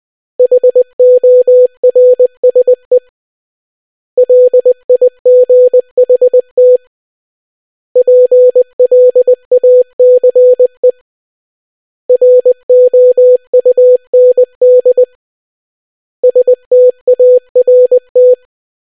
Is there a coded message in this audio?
Yes: Morse code from 0.4 to 18.3 s, heard as a keyed tone, about 20 wpm, 500 Hz. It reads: HORSE LIGHT PLACE ROUND START